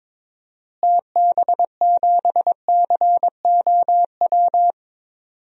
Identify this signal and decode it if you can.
Morse code, heard as a keyed tone, about 22 wpm, 700 Hz. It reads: TB7COW
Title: DXCC-100 Call Signs - 6 Characters - Repeated 2x - 22wpm